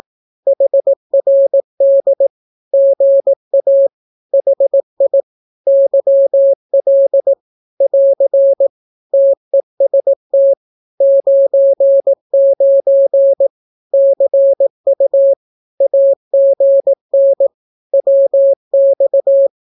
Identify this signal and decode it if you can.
Morse code, heard as a keyed tone, about 18 wpm, 550 Hz. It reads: HRD GA HI YL <AR> TEST 99 CU AGN WX
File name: naukapodstaw / skroty / skroty